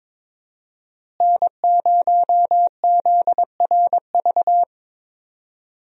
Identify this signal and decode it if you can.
Morse code, heard as a keyed tone, about 22 wpm, 700 Hz. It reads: N0ZRV
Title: USA Call Signs - 5 Characters - Repeated 3x - 22wpm